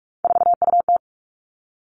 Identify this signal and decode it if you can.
Morse code, heard as a keyed tone, about 46 wpm, 700 Hz. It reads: 4UT
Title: Three Challenging Letter-Number Combinations 45wpm